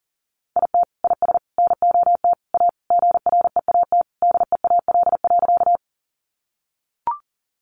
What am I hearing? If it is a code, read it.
Morse code, heard as a keyed tone, about 40 wpm, 700 Hz. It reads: IT IS NOT A GREAT DEAL.
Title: Sentences from Top 800 Words 40wpm